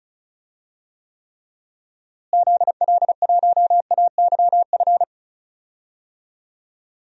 Morse code, heard as a keyed tone, about 36 wpm, 700 Hz. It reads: ZL1AYF
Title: DXCC-100 Call Signs - 6 Characters - Repeated 3x - 35wpm